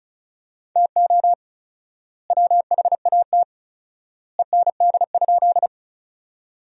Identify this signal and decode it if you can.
Morse code, heard as a keyed tone, about 36 wpm, 700 Hz. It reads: TO WHAT END?
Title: 4x Word-Spacing - Sentences from Top 200 Words - 35wpm